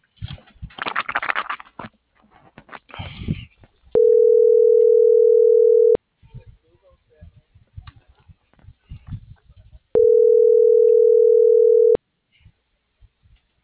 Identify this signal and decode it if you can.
Ringback tone (North America), 440 + 480 Hz.
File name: CENTRAL MAERSK / out adolfo.msk - .26426